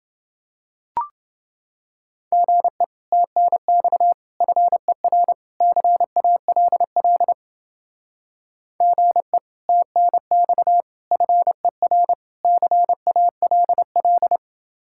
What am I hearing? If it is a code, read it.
Morse code, heard as a keyed tone, about 29 wpm, 700 Hz. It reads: GE TNX FER CALL GE TNX FER CALL